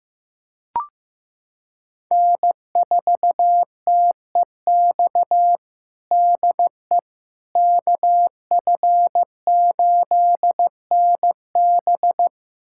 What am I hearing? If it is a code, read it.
Morse code, heard as a keyed tone, about 16 wpm, 700 Hz. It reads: N4TEX DE KF8NB
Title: USA Callsign DE Callsign - 5 Characters - 15wpm